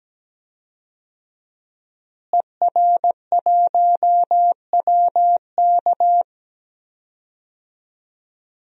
Morse code, heard as a keyed tone, about 17 wpm, 700 Hz. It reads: ER1WK
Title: DXCC-100 Call Signs - 5 Characters - Repeated 3x - 17wpm